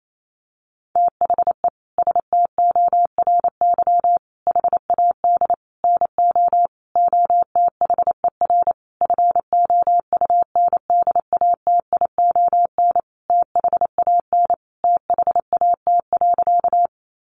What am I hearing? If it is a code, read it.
Morse code, heard as a keyed tone, about 28 wpm, 700 Hz. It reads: THE STORY HAD NO OTHER FOUNDATION THAN THAT.
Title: Sentences from Top 1200 Words 28wpm